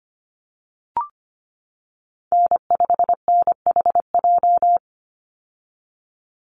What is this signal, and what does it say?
Morse code, heard as a keyed tone, about 25 wpm, 700 Hz. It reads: N5NHJ
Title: Call Signs - 25wpm